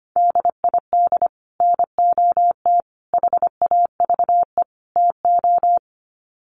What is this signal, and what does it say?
Morse code, heard as a keyed tone, about 25 wpm, 700 Hz. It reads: DID NOT HAVE TO
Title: Sets of 4 Words 25wpm